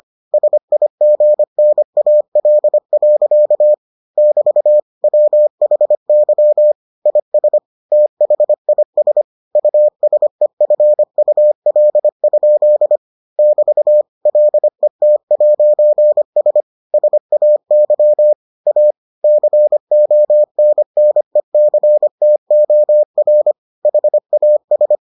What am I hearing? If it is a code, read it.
Morse code, heard as a keyed tone, about 25 wpm, 600 Hz. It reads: SIGNAL. <BT> WHY IS THIS USEFUL? <BT> LET'S SAY A CONNECTOR HAS